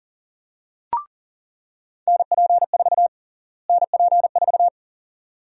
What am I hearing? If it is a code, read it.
Morse code, heard as a keyed tone, about 37 wpm, 700 Hz. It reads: NP4 NP4
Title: USA Call Sign Prefixes - Repeated 2x - 40wpm